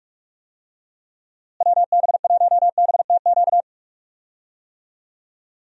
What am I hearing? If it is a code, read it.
Morse code, heard as a keyed tone, about 46 wpm, 700 Hz. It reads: WB1BTQ
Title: USA Call Signs - 6 Characters - Repeated 3x - 45wpm